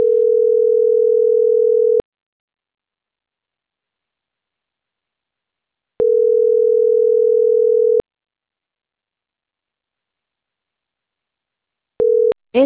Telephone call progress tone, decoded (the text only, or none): ringback tone